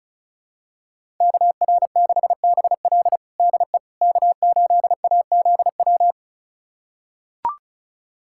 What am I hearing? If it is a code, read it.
Morse code, heard as a keyed tone, about 34 wpm, 700 Hz. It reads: KR6BL DE K8AZW
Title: USA Callsign DE Callsign - 5 Characters - 35wpm